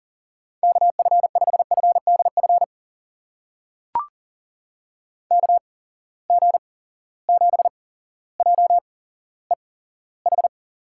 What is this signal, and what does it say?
Morse code, heard as a keyed tone, about 40 wpm, spaced as for 17 wpm, 700 Hz. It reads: KF5FDF KG7JEH
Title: USA Call Signs - 8x Character Spacing - 40wpm